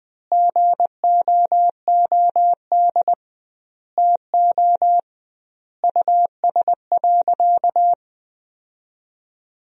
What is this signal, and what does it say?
Morse code, heard as a keyed tone, about 20 wpm, 700 Hz. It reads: GOOD TO US.
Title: 2x Word-Spacing - Sentences from Top 200 Words - 20wpm